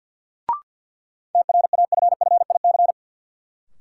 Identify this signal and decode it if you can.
Morse code, heard as a keyed tone, about 51 wpm, 700 Hz. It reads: TRAFFIC
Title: Sentences from Top 1300 Words - New Word Drill Alt - 50wpm